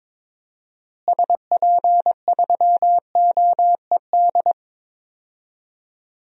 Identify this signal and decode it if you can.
Morse code, heard as a keyed tone, about 22 wpm, 700 Hz. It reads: SP3OED
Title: DXCC-100 Call Signs - 6 Characters - Repeated 3x - 22wpm